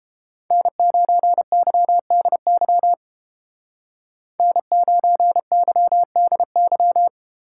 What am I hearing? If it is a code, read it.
Morse code, heard as a keyed tone, about 32 wpm, 700 Hz. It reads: N9YDY N9YDY